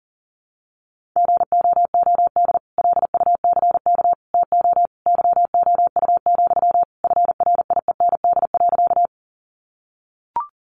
Morse code, heard as a keyed tone, about 40 wpm, 700 Hz. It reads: GOOD LUCK TO YOU, FRIEND.
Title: Sentences from Top 1500 Words 40wpm